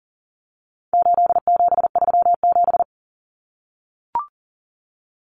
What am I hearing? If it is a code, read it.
Morse code, heard as a keyed tone, about 40 wpm, 700 Hz. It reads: 8737
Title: Common Cut Numbers - 4-Digit 40wpm